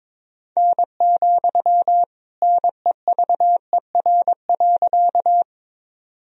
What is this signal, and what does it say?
Morse code, heard as a keyed tone, about 22 wpm, 700 Hz. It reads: N, NEVER.
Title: Sentences from Top 300 Words and Abbreviations 22wpm